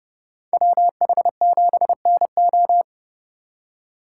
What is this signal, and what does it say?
Morse code, heard as a keyed tone, about 29 wpm, 700 Hz. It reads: WH7NO